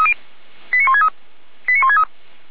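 Five-tone selective call (ZVEI), heard as quick runs of five tones, tones of about 70 ms. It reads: 89163 89163
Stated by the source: Five-tone ZVEI selective call received by SDR, fire brigade Orlamünde (feuerwehr-orlamuende).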